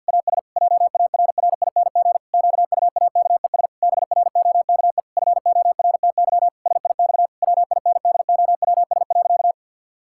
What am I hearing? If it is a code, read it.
Morse code, heard as a keyed tone, about 50 wpm, 700 Hz. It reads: AS JARRING QUAKES BROKE FORTY SIX WINDOWS.
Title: Pangram Challenge v2 - 50wpm